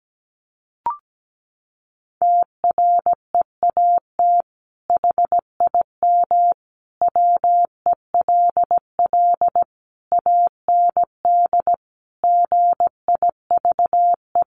Morse code, heard as a keyed tone, about 17 wpm, 700 Hz. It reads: TREAT HIM WELL AND GIVE
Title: Sentences from Top 1300 Words 17wpm